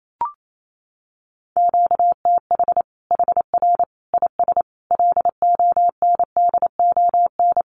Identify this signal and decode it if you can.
Morse code, heard as a keyed tone, about 28 wpm, 700 Hz. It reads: QTH HR IS LONDON
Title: Common QSO Phrases 28wpm